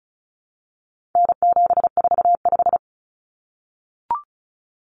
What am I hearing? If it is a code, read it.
Morse code, heard as a keyed tone, about 34 wpm, 700 Hz. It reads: N745
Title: Common Cut Numbers - 4-Digit 35wpm